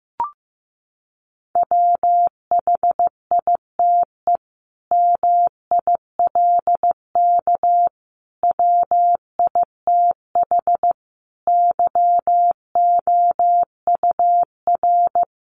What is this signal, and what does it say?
Morse code, heard as a keyed tone, about 15 wpm, 700 Hz. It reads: WHITE MILK WITH YOUR
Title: Sentences from Top 1500 Words 15wpm